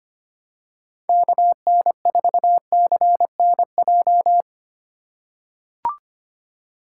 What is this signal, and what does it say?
Morse code, heard as a keyed tone, about 25 wpm, 700 Hz. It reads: KN4CNJ
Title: USA Call Signs - 6 Characters - 2x Character Spacing - 25wpm